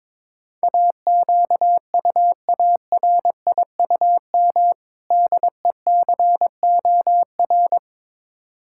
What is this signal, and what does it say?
Morse code, heard as a keyed tone, about 22 wpm, 700 Hz. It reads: AQUARIUM DECOR
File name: Difficult Words with Hints - 22wpm